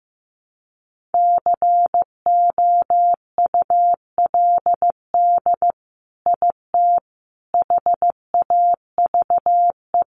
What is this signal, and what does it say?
Morse code, heard as a keyed tone, about 15 wpm, 700 Hz. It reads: COULD IT HAVE
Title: Sentences from Top 1200 Words 15wpm